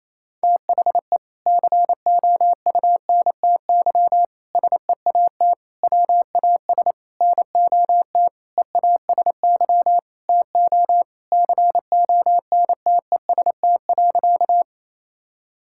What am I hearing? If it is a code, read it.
Morse code, heard as a keyed tone, about 28 wpm, 700 Hz. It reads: THE COUNTY SEAT WAS NOT EASY TO CONTEST.
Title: Sentences from Top 1600 Words 28wpm